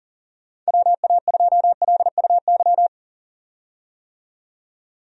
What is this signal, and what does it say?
Morse code, heard as a keyed tone, about 40 wpm, 700 Hz. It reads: WA2LUY
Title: USA Call Signs - 6 Characters - 40wpm